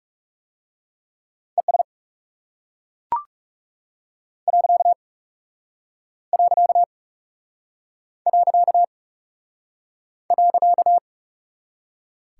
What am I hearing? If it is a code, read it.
Morse code, heard as a keyed tone, about 38 wpm, 700 Hz. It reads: ES . .